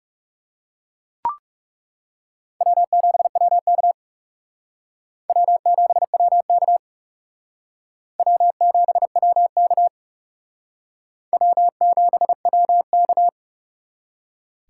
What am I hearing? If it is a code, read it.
Morse code, heard as a keyed tone, about 37 wpm, 700 Hz. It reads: W7WK W7WK W7WK W7WK
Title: Speed Racing - USA 1x2 Call Signs 30wpm